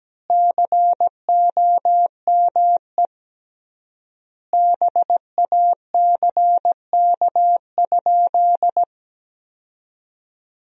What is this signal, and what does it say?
Morse code, heard as a keyed tone, about 17 wpm, 700 Hz. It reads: COME BACK?
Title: 3x Word-Spacing - Sentences from Top 200 Words - 17wpm